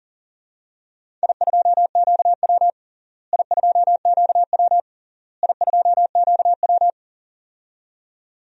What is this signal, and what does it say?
Morse code, heard as a keyed tone, about 40 wpm, 700 Hz. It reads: I2QW I2QW I2QW
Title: DXCC-100 Call Signs - 4 Characters - Repeated 3x - 40wpm